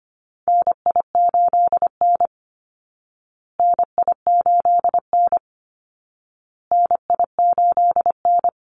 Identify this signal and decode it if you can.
Morse code, heard as a keyed tone, about 25 wpm, 700 Hz. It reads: NI8N NI8N NI8N
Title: Call Signs Repeated 2x - 25wpm